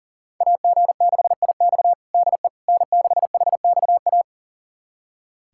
Morse code, heard as a keyed tone, about 37 wpm, 700 Hz. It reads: AG6IX DE N6HXA
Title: USA Callsign DE Callsign - 5 Characters - 40wpm